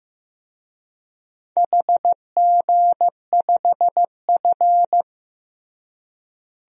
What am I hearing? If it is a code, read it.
Morse code, heard as a keyed tone, about 16 wpm, 700 Hz. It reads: HG5F
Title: DXCC-100 Call Signs - 4 Characters - Repeated 3x - 15wpm